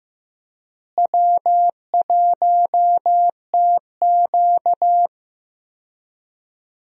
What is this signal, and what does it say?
Morse code, heard as a keyed tone, about 15 wpm, 700 Hz. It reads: W1TQ
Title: USA Call Signs - 4 Characters - Repeated 2x - 15wpm